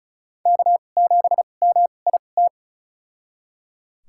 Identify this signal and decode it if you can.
Morse code, heard as a keyed tone, about 34 wpm, spaced as for 25 wpm, 700 Hz. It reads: K7MIT
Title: USA Call Signs - 5 Characters - 2x Character Spacing - 35wpm